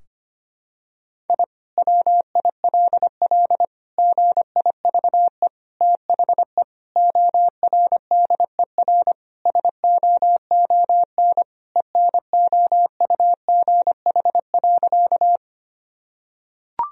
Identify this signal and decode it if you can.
Morse code, heard as a keyed tone, about 25 wpm, 700 Hz. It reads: I WILL GIVE THE ORDER SOON ENOUGH.
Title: Sentences from Top 1000 Words - Review Part 2 - 25wpm